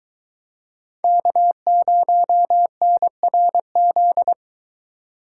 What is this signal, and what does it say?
Morse code, heard as a keyed tone, about 22 wpm, 700 Hz. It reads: K0NRZ